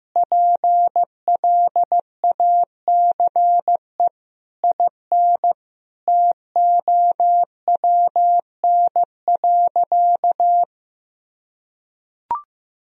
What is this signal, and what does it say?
Morse code, heard as a keyed tone, about 15 wpm, 700 Hz. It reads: PLACE IN TOWN.
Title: Sentences from Top 1000 Words - Review Part 2 - 15wpm